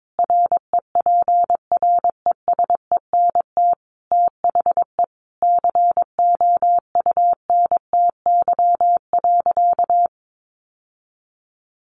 Morse code, heard as a keyed tone, about 22 wpm, 700 Hz. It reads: REPRESENT THE COUNTY.
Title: Sentences from Top 1200 Words 22wpm